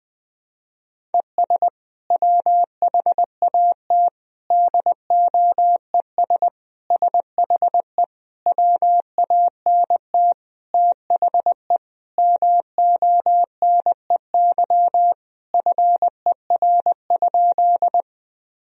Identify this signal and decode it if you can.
Morse code, heard as a keyed tone, about 20 wpm, 700 Hz. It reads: ES WHAT DOES SHE WANT THE MONEY FER?
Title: Sentences from Top 400 Words and Abbreviations 20wpm